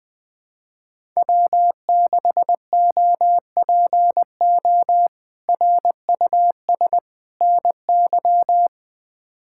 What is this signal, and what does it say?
Morse code, heard as a keyed tone, about 20 wpm, 700 Hz. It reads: W6OPO RUS NY
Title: K1USN SST QSO Practice - 20wpm